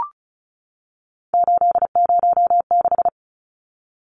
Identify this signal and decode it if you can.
Morse code, heard as a keyed tone, about 34 wpm, 700 Hz. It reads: 806
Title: Numbers - 3-Digit 35wpm